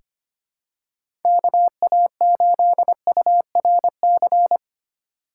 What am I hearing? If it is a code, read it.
Morse code, heard as a keyed tone, about 25 wpm, 700 Hz. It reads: KA8URC
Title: USA Call Signs - 6 Characters - 2x Character Spacing - 25wpm